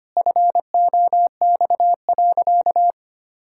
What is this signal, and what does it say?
Morse code, heard as a keyed tone, about 25 wpm, 700 Hz. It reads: FOX.
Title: Pangram Challenge v2 - 25wpm